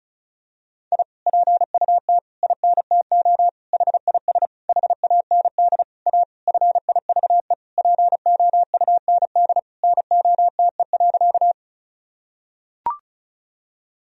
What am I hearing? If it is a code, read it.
Morse code, heard as a keyed tone, about 34 wpm, 700 Hz. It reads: I PUT INTO HIS HAND A FIVE POUND NOTE.
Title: Sentences from Top 1000 Words - Review Part 2 - 35wpm